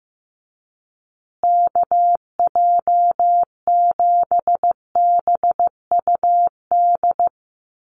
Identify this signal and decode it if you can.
Morse code, heard as a keyed tone, about 15 wpm, 700 Hz. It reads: KJ7BUD
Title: USA Call Signs Repeated 2x 15wpm